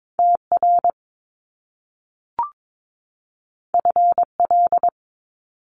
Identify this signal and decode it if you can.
Morse code, heard as a keyed tone, about 22 wpm, 700 Hz. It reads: TR FL
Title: Top Two Letter Combinations in Words 22wpm